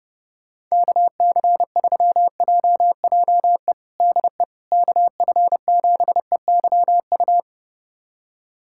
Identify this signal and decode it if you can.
Morse code, heard as a keyed tone, about 30 wpm, 700 Hz. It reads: KC3JJE DE KF7EYU
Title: USA Callsign DE Callsign 30wpm